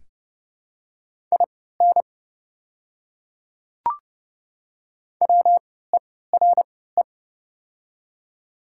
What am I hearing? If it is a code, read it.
Morse code, heard as a keyed tone, about 30 wpm, spaced as for 17 wpm, 700 Hz. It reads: IN WERE
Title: 3x Character Spacing - Top 300 Words - 30wpm